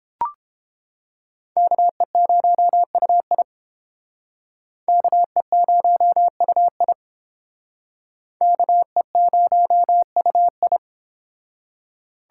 Morse code, heard as a keyed tone, about 30 wpm, 700 Hz. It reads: KE0UI KE0UI KE0UI